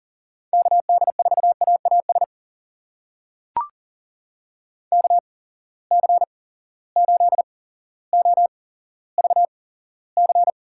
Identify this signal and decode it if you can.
Morse code, heard as a keyed tone, about 40 wpm, spaced as for 17 wpm, 700 Hz. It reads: KD4AAS KC8OVC
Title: USA Call Signs - 6 Characters - 8x Character Spacing - 40wpm